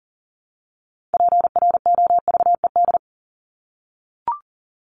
Morse code, heard as a keyed tone, about 40 wpm, 700 Hz. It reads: PROVED
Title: ED Suffix - Top 100 Words with Base Words 40wpm